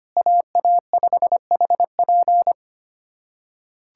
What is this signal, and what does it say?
Morse code, heard as a keyed tone, about 25 wpm, 700 Hz. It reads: AA5HP
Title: USA Callsign DE Callsign - 5 Characters - 25wpm